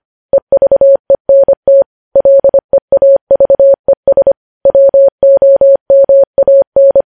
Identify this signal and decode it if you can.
Morse code, heard as a keyed tone, about 25 wpm, 550 Hz. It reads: EVENT LEAVES WOMAN
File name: podcast / News headlines-1x-25x25